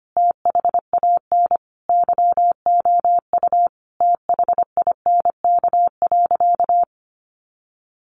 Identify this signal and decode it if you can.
Morse code, heard as a keyed tone, about 25 wpm, 700 Hz. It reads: THAN YOU THINK.
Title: Sentences from Top 600 Words 25wpm